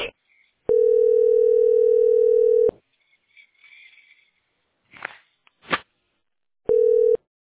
Ringback tone (North America), 440 + 480 Hz.